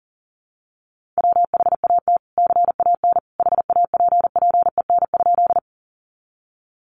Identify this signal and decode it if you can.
Morse code, heard as a keyed tone, about 40 wpm, 700 Hz. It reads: WHAT CAN HAPPEN?
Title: Sentences from Top 400 Words 40wpm